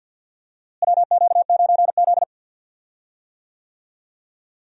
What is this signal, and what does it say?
Morse code, heard as a keyed tone, about 53 wpm, 700 Hz. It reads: WQ9Z
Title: USA Call Signs - 4 Characters - Repeated 3x - 50wpm